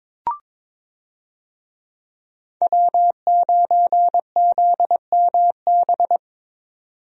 Morse code, heard as a keyed tone, about 21 wpm, 700 Hz. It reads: W9ZMB
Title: USA Call Signs - 5 Characters - 22wpm